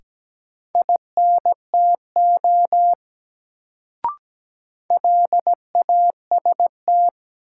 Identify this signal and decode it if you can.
Morse code, heard as a keyed tone, about 17 wpm, 700 Hz. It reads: INTO LAST
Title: Sets of 2 Words - New Word Drill - 17wpm